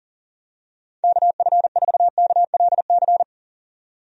Morse code, heard as a keyed tone, about 40 wpm, 700 Hz. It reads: KF4KLC
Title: USA Call Signs - 6 Characters - 40wpm